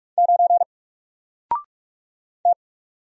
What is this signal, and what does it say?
Morse code, heard as a keyed tone, about 46 wpm, 700 Hz. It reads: 9 T
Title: Single Letter-Number Alt 45wpm